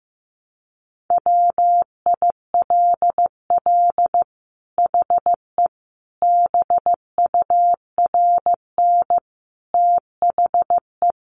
Morse code, heard as a keyed tone, about 15 wpm, 700 Hz. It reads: WILL HE BURN THE
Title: Sentences from Top 1300 Words 15wpm